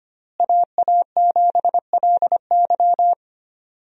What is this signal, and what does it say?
Morse code, heard as a keyed tone, about 25 wpm, 700 Hz. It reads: AA7LY
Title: USA Call Signs - 5 Characters - Repeated 3x - 25wpm